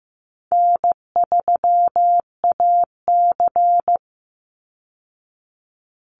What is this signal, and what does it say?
Morse code, heard as a keyed tone, about 15 wpm, 700 Hz. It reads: N3AC
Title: Easy Call Signs Repeated 3x - 15wpm